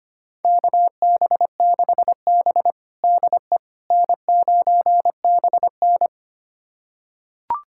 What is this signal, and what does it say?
Morse code, heard as a keyed tone, about 25 wpm, 700 Hz. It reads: KB6B DE N9BN
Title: USA Callsign DE Callsign - 4 Characters - 25wpm